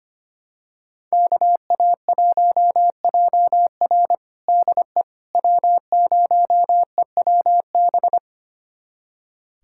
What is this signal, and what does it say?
Morse code, heard as a keyed tone, about 25 wpm, 700 Hz. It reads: KA1JR DE W0EWB
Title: USA Callsign DE Callsign - 5 Characters - 25wpm